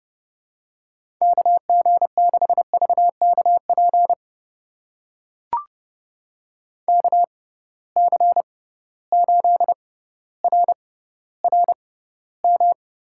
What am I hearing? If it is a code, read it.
Morse code, heard as a keyed tone, about 30 wpm, spaced as for 15 wpm, 700 Hz. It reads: KG6VKP KC8RRM
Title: USA Call Signs - 6 Characters - 6x Character Spacing - 30wpm